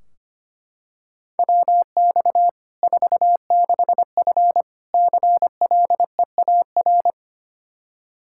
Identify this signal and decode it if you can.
Morse code, heard as a keyed tone, about 25 wpm, 700 Hz. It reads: WX 46F CLEAR
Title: Practice QSO - Exchange 2 - Info - 25wpm